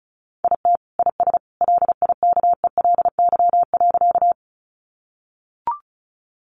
Morse code, heard as a keyed tone, about 34 wpm, 700 Hz. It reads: IT IS LIKELY.
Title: Sentences from Top 600 Words 35wpm